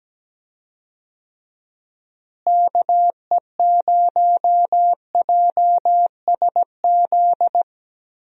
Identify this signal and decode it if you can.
Morse code, heard as a keyed tone, about 17 wpm, 700 Hz. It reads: KE0JSZ
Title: USA Call Signs - 6 Characters - Repeated 2x - 17wpm